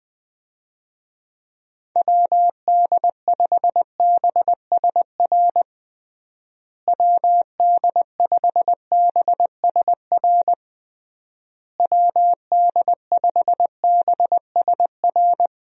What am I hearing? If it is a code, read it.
Morse code, heard as a keyed tone, about 20 wpm, 700 Hz. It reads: WD5BSR WD5BSR WD5BSR
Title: USA Call Signs - 6 Characters - Repeated 3x - 20wpm